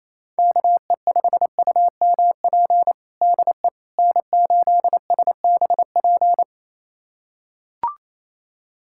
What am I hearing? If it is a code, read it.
Morse code, heard as a keyed tone, about 28 wpm, 700 Hz. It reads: KE5UMP DE N8SBP
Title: USA Callsign DE Callsign 28wpm